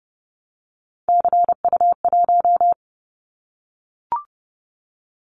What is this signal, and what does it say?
Morse code, heard as a keyed tone, about 30 wpm, 700 Hz. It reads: CU1
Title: Callsign Prefixes for Top 50 DXCC - 30wpm